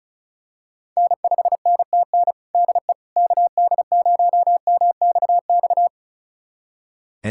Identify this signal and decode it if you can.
Morse code, heard as a keyed tone, about 34 wpm, 700 Hz. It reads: N5NTN DE KD0MXX
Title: USA Callsign DE Callsign 35wpm